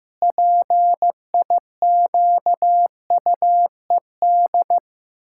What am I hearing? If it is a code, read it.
Morse code, heard as a keyed tone, about 15 wpm, 700 Hz. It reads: PIQUED
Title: Pangram Challenge v2 - 15wpm